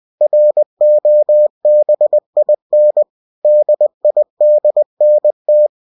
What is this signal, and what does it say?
Morse code, heard as a keyed tone, about 20 wpm, 600 Hz. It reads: ROBIN DIDNT
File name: cw-books / winnie-the-pooh / winnie-20wpm-page-0079